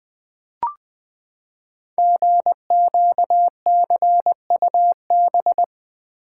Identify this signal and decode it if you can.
Morse code, heard as a keyed tone, about 20 wpm, 700 Hz. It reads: GQCUB
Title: Code Groups - 5 Letter:Numbers - 20wpm